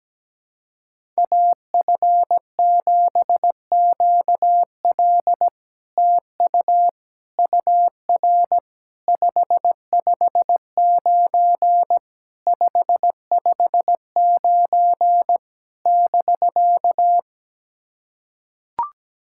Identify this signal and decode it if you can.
Morse code, heard as a keyed tone, about 17 wpm, 700 Hz. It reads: AF7QL TU UR 559 559 <BK>
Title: SOTA QSO Practice 17wpm